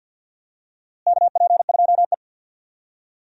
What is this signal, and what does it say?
Morse code, heard as a keyed tone, about 49 wpm, 700 Hz. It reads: KP2E
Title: DXCC-100 Call Signs - 4 Characters - Repeated 3x - 50wpm